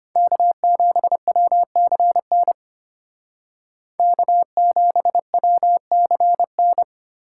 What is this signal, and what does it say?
Morse code, heard as a keyed tone, about 32 wpm, 700 Hz. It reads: K7WCN K7WCN